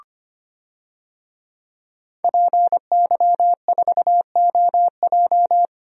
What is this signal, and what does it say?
Morse code, heard as a keyed tone, about 25 wpm, 700 Hz. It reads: PY4OJ
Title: DXCC-100 Call Signs - 5 Characters - Repeated 3x - 25wpm